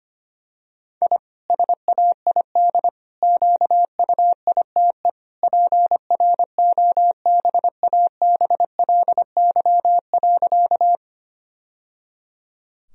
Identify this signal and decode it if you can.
Morse code, heard as a keyed tone, about 25 wpm, 700 Hz. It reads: I SAID QUITE PROBABLY.
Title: Sentences from Top 1000 Words - Review Part 4 - 25wpm